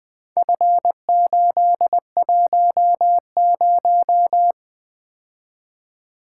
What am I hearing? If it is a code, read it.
Morse code, heard as a keyed tone, about 20 wpm, 700 Hz. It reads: F810
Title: Four Letter-Number Combinations - 20wpm